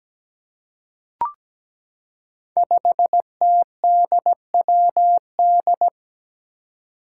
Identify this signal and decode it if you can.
Morse code, heard as a keyed tone, about 17 wpm, 700 Hz. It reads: 5TDWD
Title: Five Challenging Letter-Number Combinations 17wpm